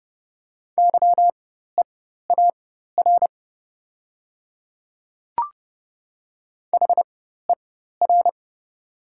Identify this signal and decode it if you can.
Morse code, heard as a keyed tone, about 30 wpm, spaced as for 14 wpm, 690 Hz. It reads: YEAR HER
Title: 4x Character Spacing - Top 500 Words - 30wpm